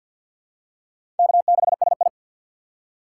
Morse code, heard as a keyed tone, about 47 wpm, 700 Hz. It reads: K6SI
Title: USA Call Signs - 4 Characters - 3x Character Spacing - 50wpm